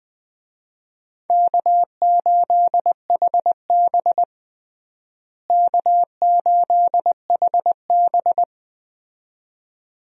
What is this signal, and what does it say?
Morse code, heard as a keyed tone, about 20 wpm, 700 Hz. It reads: K8HB K8HB
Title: USA Call Signs - 4 Characters - Repeated 2x - 20wpm